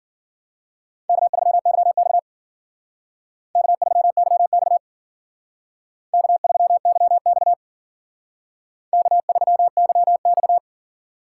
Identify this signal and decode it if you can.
Morse code, heard as a keyed tone, about 51 wpm, 700 Hz. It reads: K3YX K3YX K3YX K3YX